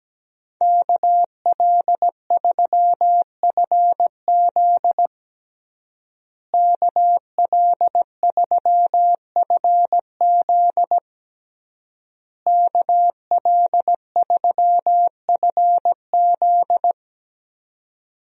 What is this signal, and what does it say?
Morse code, heard as a keyed tone, about 17 wpm, 700 Hz. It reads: KL3FZ KL3FZ KL3FZ
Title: USA Call Signs - 5 Characters - Repeated 3x - 17wpm